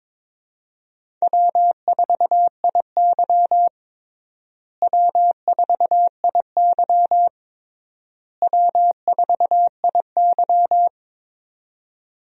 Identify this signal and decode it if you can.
Morse code, heard as a keyed tone, about 22 wpm, 700 Hz. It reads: W4IY W4IY W4IY
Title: USA Call Signs - 4 Characters - Repeated 3x - 22wpm